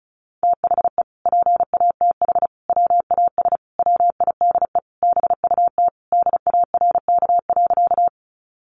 Morse code, heard as a keyed tone, about 34 wpm, 700 Hz. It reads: THE PATH WAS WIDE BUT DARK.
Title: Sentences from Top 1400 Words 35wpm